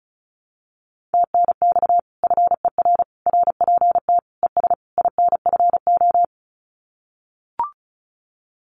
Morse code, heard as a keyed tone, about 36 wpm, 700 Hz. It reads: TNX FER RPT ES INFO
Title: Common QSO Phrases Encore 35wpm